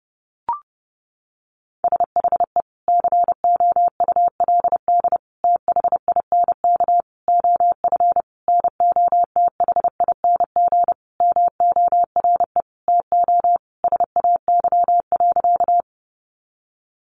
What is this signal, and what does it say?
Morse code, heard as a keyed tone, about 30 wpm, 700 Hz. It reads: SHE COULD THINK OF NOTHING MORE TO SAY.
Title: Sentences from Top 500 Words 30wpm